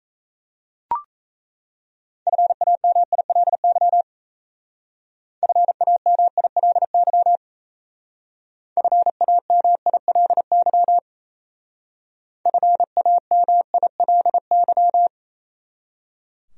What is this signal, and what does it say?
Morse code, heard as a keyed tone, about 34 wpm, 700 Hz. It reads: FAMILY FAMILY FAMILY FAMILY
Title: Speed-Racing Top 500 Words 28wpm